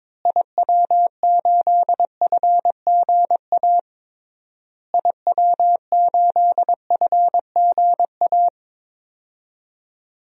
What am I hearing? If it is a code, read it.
Morse code, heard as a keyed tone, about 22 wpm, 700 Hz. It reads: IW8FGA IW8FGA
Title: DXCC-100 Call Signs - 6 Characters - Repeated 3x - 22wpm